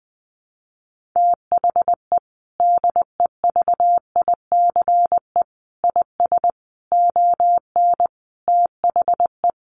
Morse code, heard as a keyed tone, about 20 wpm, 700 Hz. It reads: THE DEVICE IS ON THE
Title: Sentences from Top 1500 Words 20wpm